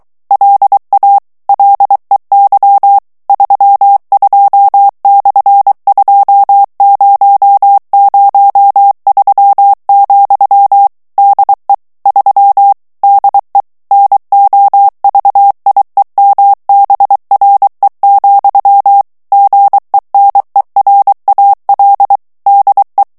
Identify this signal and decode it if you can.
Morse code, heard as a keyed tone, about 23 wpm, 800 Hz. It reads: LA LEY 32/2003, DE 3 DE NOVIEMBRE, GENERAL DE
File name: ESPAGNOL TEXTE EN CLAIR 2 VITESSE 22 MOTS MINUTES